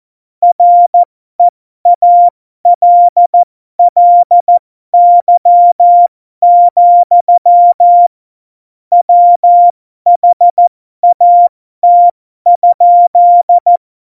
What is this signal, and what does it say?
Morse code, heard as a keyed tone, about 14 wpm, spaced as for 12 wpm, 700 Hz. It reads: REALLY, WHAT?
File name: Morsekurs / Witze